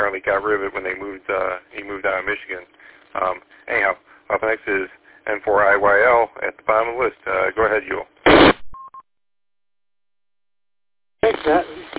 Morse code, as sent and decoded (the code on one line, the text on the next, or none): none